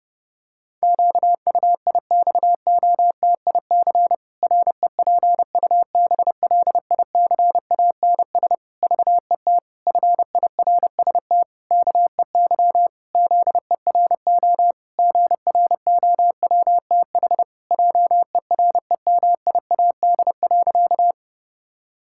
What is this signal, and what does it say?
Morse code, heard as a keyed tone, about 30 wpm, 700 Hz. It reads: QUIXOTIC REPUBLICANS VET FIRST KEY ZERO GROWTH JEREMIAD.
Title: Pangram Challenge v2 - 30wpm